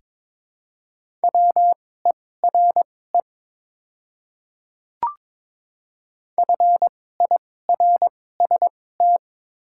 Morse code, heard as a keyed tone, about 22 wpm, spaced as for 16 wpm, 700 Hz. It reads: WERE FIRST
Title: 2x Character Spacing - Top 100 Words - 22wpm